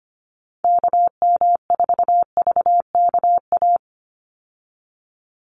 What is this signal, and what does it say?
Morse code, heard as a keyed tone, about 25 wpm, 700 Hz. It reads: KM4VKA
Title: USA Call Signs 25wpm